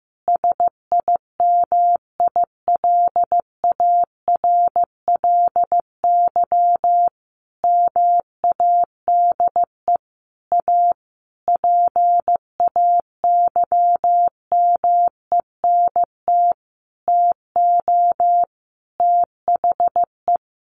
Morse code, heard as a keyed tone, about 15 wpm, 700 Hz. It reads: SIMILARLY MADE A PAYMENT TO THE
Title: Sentences from Top 1400 Words 15wpm